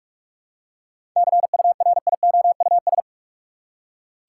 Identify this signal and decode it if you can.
Morse code, heard as a keyed tone, about 43 wpm, 700 Hz. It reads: CURIOUS
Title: Seven Letter Words - 45wpm